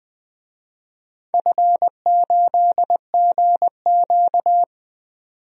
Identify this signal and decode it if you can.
Morse code, heard as a keyed tone, about 20 wpm, 700 Hz. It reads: F8GQ
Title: DXCC-100 Call Signs - 4 Characters - 20wpm